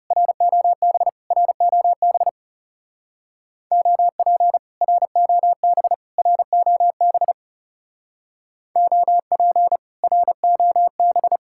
Morse code, heard as a keyed tone, about 35 wpm, 700 Hz. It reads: ROB ROB OP ROB ROB OP ROB
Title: Speed-Racing Common QSO Phrases Encore - 30wpm